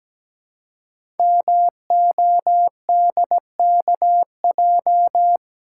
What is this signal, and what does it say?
Morse code, heard as a keyed tone, about 17 wpm, 700 Hz. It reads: MODKJ
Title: Code Groups - 5 Letter:Numbers - 17wpm